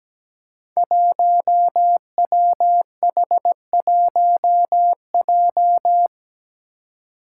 Morse code, heard as a keyed tone, about 17 wpm, 700 Hz. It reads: 1WH1J
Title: Five Challenging Letter-Number Combinations 17wpm